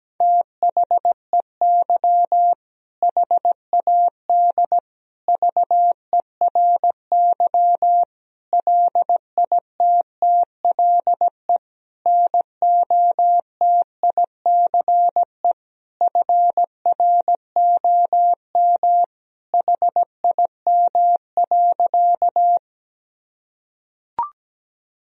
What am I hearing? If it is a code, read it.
Morse code, heard as a keyed tone, about 17 wpm, 700 Hz. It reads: THEY HAD VERY LITTLE NOTICE FROM HIM.
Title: Sentences from Top 1000 Words - Review Part 2 - 17wpm